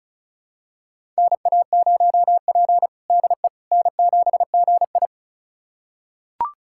Morse code, heard as a keyed tone, about 36 wpm, 700 Hz. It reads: NA0P DE N7GI
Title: USA Callsign DE Callsign - 4 Characters - 35wpm